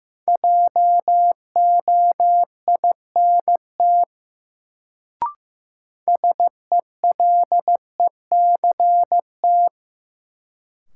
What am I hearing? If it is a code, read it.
Morse code, heard as a keyed tone, about 15 wpm, 700 Hz. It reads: JOINT SELECT
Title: Sentences from Top 1300 Words - New Word Drill - 15wpm